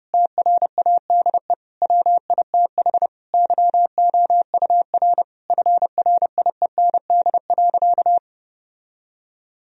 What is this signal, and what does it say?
Morse code, heard as a keyed tone, about 32 wpm, 700 Hz. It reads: TRADE WITH YOUR FRIEND.
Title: Sentences from Top 1000 Words - Review Part 4 - 30wpm